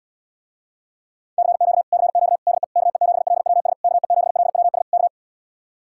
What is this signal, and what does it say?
Morse code, heard as a keyed tone, about 78 wpm, 700 Hz. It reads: CQ CQ DE KE7LWS KE7LWS K